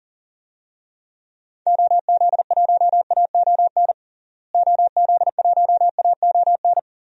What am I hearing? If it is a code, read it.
Morse code, heard as a keyed tone, about 40 wpm, 700 Hz. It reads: OZ1AON OZ1AON
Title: DXCC-100 Call Signs - 6 Characters - Repeated 3x - 40wpm